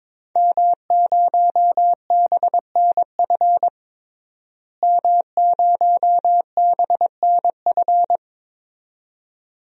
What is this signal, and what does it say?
Morse code, heard as a keyed tone, about 22 wpm, 700 Hz. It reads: M0BNF M0BNF
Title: DXCC-100 Call Signs - 5 Characters - Repeated 2x - 22wpm